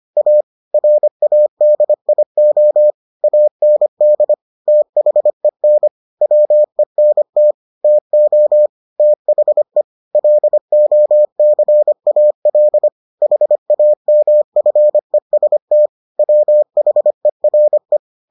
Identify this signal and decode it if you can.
Morse code, heard as a keyed tone, about 25 wpm, 600 Hz. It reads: A RADIO AND THEN WENT TO THE LOCAL HAMFEST WHERE